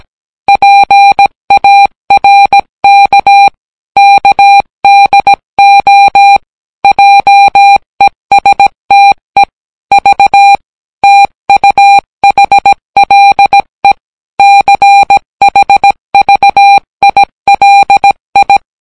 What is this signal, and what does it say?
Morse code, heard as a keyed tone, about 18 wpm, 800 Hz. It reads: PARK KDO JESTE V TUHLE CHVILI